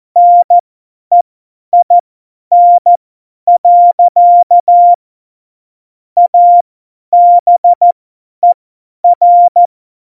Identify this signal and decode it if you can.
Morse code, heard as a keyed tone, about 14 wpm, spaced as for 10 wpm, 700 Hz. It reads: NEIN. ABER
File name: Morsekurs / Witze / Witz 03. November 2020, 50 F70